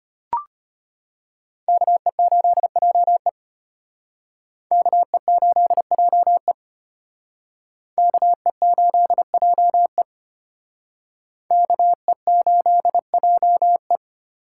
Morse code, heard as a keyed tone, about 32 wpm, 700 Hz. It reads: KE8JE KE8JE KE8JE KE8JE